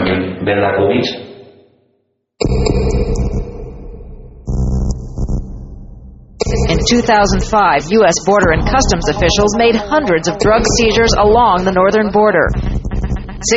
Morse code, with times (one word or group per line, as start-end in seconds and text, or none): none